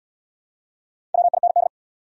Morse code, heard as a keyed tone, about 78 wpm, 700 Hz. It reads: WIND